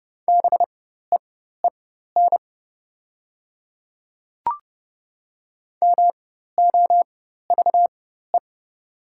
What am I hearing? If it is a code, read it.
Morse code, heard as a keyed tone, about 30 wpm, spaced as for 14 wpm, 690 Hz. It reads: BEEN MOVE
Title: 4x Character Spacing - Top 300 Words - 30wpm